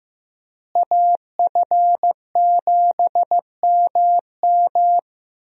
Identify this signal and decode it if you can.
Morse code, heard as a keyed tone, about 15 wpm, 700 Hz. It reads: AF7MM
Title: USA Call Signs - 5 Characters - Repeated 2x - 15wpm